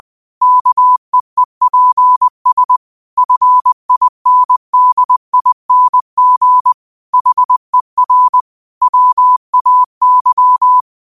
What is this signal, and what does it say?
Morse code, heard as a keyed tone, about 20 wpm, 1000 Hz. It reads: KEEPS FINDING HER WAY
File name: qotdfiles / 20wpm